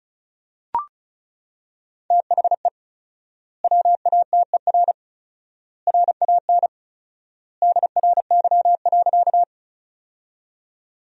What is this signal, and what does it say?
Morse code, heard as a keyed tone, about 36 wpm, 700 Hz. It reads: THE WATER RAN DRY.
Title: 4x Word-Spacing - Sentences from Top 500 Words - 35wpm